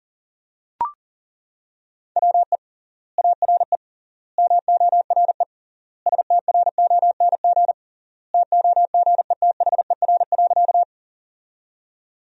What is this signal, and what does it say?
Morse code, heard as a keyed tone, about 40 wpm, 700 Hz. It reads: WE ARE MORE STRONG TOGETHER.
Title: 3x Word-Spacing - Sentences from Top 500 Words - 40wpm